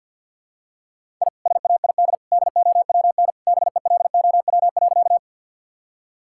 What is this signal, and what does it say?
Morse code, heard as a keyed tone, about 50 wpm, 700 Hz. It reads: I SAID DOWN BELOW.
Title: Sentences from Top 1000 Words - Review Part 5 - 50wpm